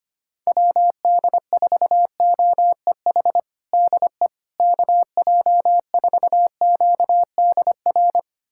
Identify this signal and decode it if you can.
Morse code, heard as a keyed tone, about 25 wpm, 700 Hz. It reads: WD4OEH DE KJ4QDR
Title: USA Callsign DE Callsign - 6 Characters - 25wpm